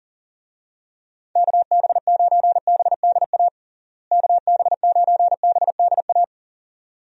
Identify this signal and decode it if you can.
Morse code, heard as a keyed tone, about 40 wpm, 700 Hz. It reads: KB9BDA KB9BDA
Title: USA Call Signs - 6 Characters - Repeated 2x - 40wpm